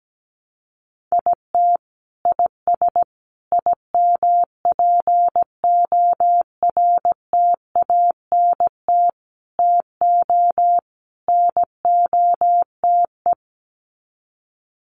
Morse code, heard as a keyed tone, about 17 wpm, 700 Hz. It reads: IT IS IMPORTANT TO NOTE
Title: Sets of 5 Words 17wpm